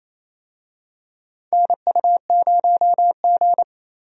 Morse code, heard as a keyed tone, about 28 wpm, 700 Hz. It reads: NU0G
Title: USA Call Signs - 4 Characters - Repeated 2x - 28wpm